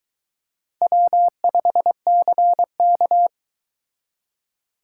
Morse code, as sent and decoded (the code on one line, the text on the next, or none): .-- ..... -.-. -.-
W5CK